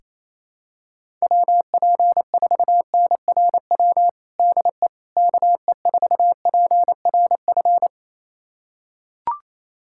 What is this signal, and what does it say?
Morse code, heard as a keyed tone, about 27 wpm, 700 Hz. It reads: WP4NRW DE KE4PRF
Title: USA Callsign DE Callsign - 6 Characters - 28wpm